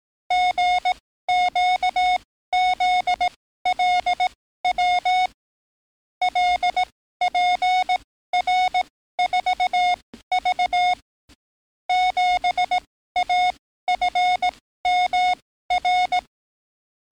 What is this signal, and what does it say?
Morse code, heard as a keyed tone, about 18 wpm, spaced as for 14 wpm, 730 Hz. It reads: GQZLW LPR4V 7AFMR
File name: training / NAVPERS 11049to11060 / NAVPERS 11056 RA RB